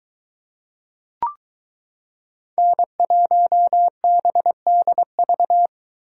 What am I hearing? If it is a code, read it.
Morse code, heard as a keyed tone, about 23 wpm, 700 Hz. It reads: N1BDV